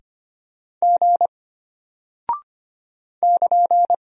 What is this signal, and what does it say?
Morse code, heard as a keyed tone, about 25 wpm, 700 Hz. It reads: G <KN>
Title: Single Character (Letter:Number:Punctuation:Prosign) Alt 25wpm